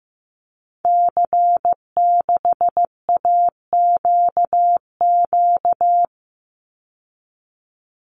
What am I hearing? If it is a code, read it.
Morse code, heard as a keyed tone, about 15 wpm, 700 Hz. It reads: C6AQQ
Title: Call Signs Repeated 2x - 15wpm